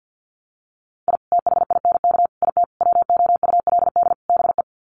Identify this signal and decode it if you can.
Morse code, heard as a keyed tone, about 50 wpm, 700 Hz. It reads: I THINK IT WOULD BE
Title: Sets of 5 Words 50wpm